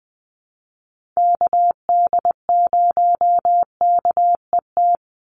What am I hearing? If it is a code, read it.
Morse code, heard as a keyed tone, about 20 wpm, 700 Hz. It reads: KD0KET
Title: USA Call Signs 20wpm